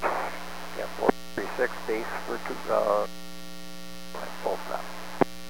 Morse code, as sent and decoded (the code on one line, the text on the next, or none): none